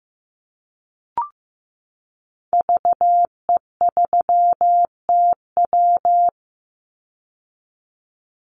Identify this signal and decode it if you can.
Morse code, heard as a keyed tone, about 15 wpm, 700 Hz. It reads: VE3TW
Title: Call Signs Repeated 3x - 15wpm